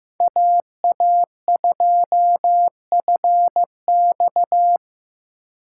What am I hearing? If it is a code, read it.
Morse code, heard as a keyed tone, about 15 wpm, 700 Hz. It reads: AA2FX
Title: USA Call Signs - 5 Characters - 6x Character Spacing - 15wpm